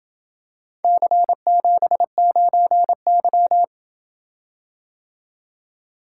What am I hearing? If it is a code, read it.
Morse code, heard as a keyed tone, about 27 wpm, 700 Hz. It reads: C79Y